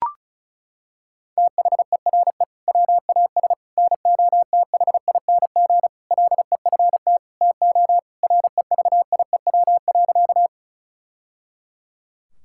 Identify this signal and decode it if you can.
Morse code, heard as a keyed tone, about 36 wpm, 700 Hz. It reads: THERE WAS NOTHING LEFT TO REVIEW.
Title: Sentences from Top 1000 Words - Review Part 6 - 35wpm